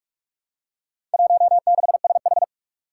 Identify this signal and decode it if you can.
Morse code, heard as a keyed tone, about 46 wpm, 700 Hz. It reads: 16SH
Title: Four Challenging Letter-Number Combinations 45wpm